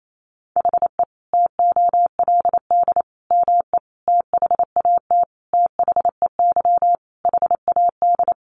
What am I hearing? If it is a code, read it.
Morse code, heard as a keyed tone, about 28 wpm, 700 Hz. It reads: HE TOLD ME THAT THEY HAD
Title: Sentences from Top 1300 Words 28wpm